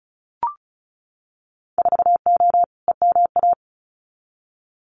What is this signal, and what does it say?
Morse code, heard as a keyed tone, about 34 wpm, 700 Hz. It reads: 4O EMA
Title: Winter Field Day Exchanges Encore 35wpm